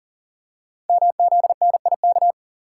Morse code, heard as a keyed tone, about 42 wpm, 700 Hz. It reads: MZNIK
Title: Code Groups - 5 Letters - 40wpm